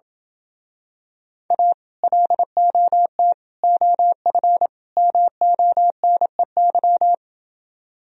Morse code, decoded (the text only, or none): A LOT OF MONEY